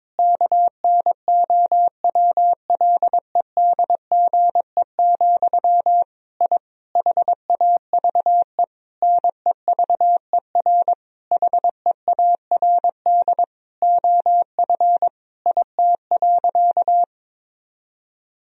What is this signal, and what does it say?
Morse code, heard as a keyed tone, about 21 wpm, 700 Hz. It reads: KNOWLEDGE, I HAVE NEVER HEARD OF IT.
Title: Sentences from Top 1000 Words - Review Part 3 - 22wpm